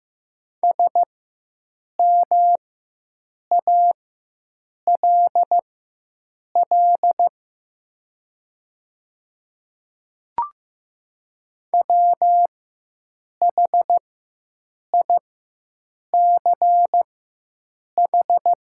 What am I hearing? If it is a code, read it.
Morse code, heard as a keyed tone, about 15 wpm, spaced as for 7 wpm, 700 Hz. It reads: SMALL WHICH
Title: 4x Character Spacing - Top 300 Words - 15wpm